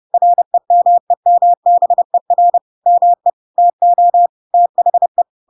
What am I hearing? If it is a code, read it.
Morse code, heard as a keyed tone, about 30 wpm, 700 Hz. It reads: REMEMBER ME TO THE
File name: Pinocchio / Pinocchio ch7 30 30 ik1hge